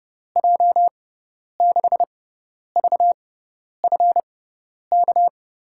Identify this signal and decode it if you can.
Morse code, heard as a keyed tone, about 31 wpm, spaced as for 10 wpm, 700 Hz. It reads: J6VFK
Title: USA Call Signs - 6x Character Spacing - 30wpm